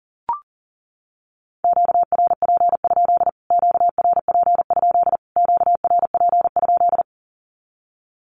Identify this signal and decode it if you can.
Morse code, heard as a keyed tone, about 40 wpm, 700 Hz. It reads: QRP? QRP? QRP?
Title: Q-Codes Repeated 3x 40wpm